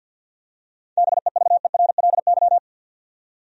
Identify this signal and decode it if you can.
Morse code, heard as a keyed tone, about 51 wpm, 700 Hz. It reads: BEVERLY
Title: USA Common Names - 50wpm